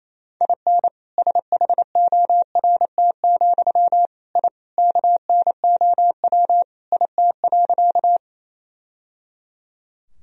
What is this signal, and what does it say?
Morse code, heard as a keyed tone, about 28 wpm, 700 Hz. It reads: IN SHORT, I KNOW IT.
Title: Sentences from Top 1000 Words - Review Part 2 - 28wpm